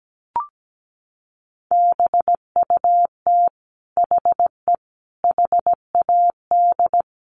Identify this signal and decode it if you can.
Morse code, heard as a keyed tone, about 17 wpm, 700 Hz. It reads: BUT HE HAD
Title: Sentences from Top 100 Words 17wpm